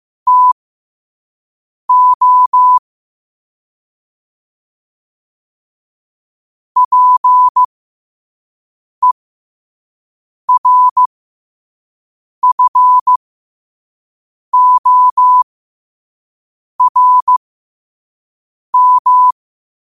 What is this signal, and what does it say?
Morse code, heard as a keyed tone, about 15 wpm, spaced as for 5 wpm, 1000 Hz. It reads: TO PERFORM
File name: qotdfiles / 5wpm